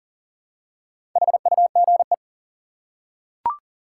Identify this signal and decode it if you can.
Morse code, heard as a keyed tone, about 39 wpm, 700 Hz. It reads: HUGE
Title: Four Letter Words - 40wpm